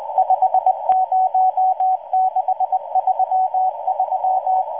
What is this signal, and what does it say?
Morse code, heard as a keyed tone, about 20 wpm, 750 Hz. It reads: 50633